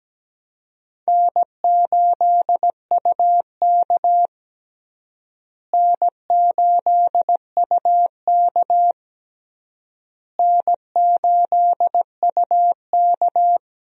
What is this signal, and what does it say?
Morse code, heard as a keyed tone, about 17 wpm, 700 Hz. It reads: N8UK N8UK N8UK
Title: USA Call Signs - 4 Characters - Repeated 3x - 17wpm